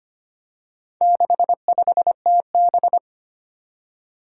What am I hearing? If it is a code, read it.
Morse code, heard as a keyed tone, about 25 wpm, 700 Hz. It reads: 65TB
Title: Four Challenging Letter-Number Combinations 25wpm